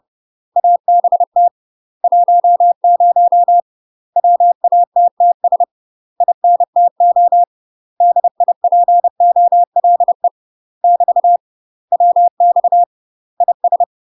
Morse code, heard as a keyed tone, about 30 wpm, 700 Hz. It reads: ABT 10 WATTS INTO DIPOLE <BT> WX IS
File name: UK MASTER QSOs TRAINING / 30WPM / UK MASTER QSOs 0049